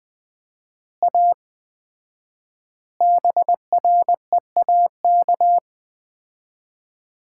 Morse code, heard as a keyed tone, about 20 wpm, 700 Hz. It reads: A BREAK
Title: 4x Word-Spacing - Sets of 4 Words - 20wpm